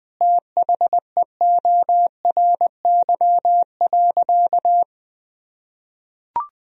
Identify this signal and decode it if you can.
Morse code, heard as a keyed tone, about 20 wpm, 700 Hz. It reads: THEORY.
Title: Sentences from Top 1000 Words - Review Part 3 - 20wpm